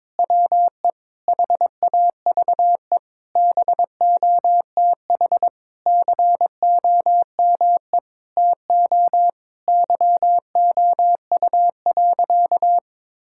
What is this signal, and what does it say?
Morse code, heard as a keyed tone, about 22 wpm, 700 Hz. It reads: WE HAVE BOTH COME TO YOU.
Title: Sentences from Top 1600 Words 22wpm